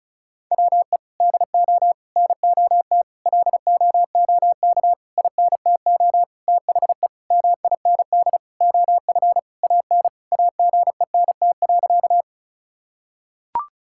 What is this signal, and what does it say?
Morse code, heard as a keyed tone, about 36 wpm, 700 Hz. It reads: WE DO NOT LOOK INTO THE MIND OF AN AGENT.
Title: Sentences from Top 1600 Words 35wpm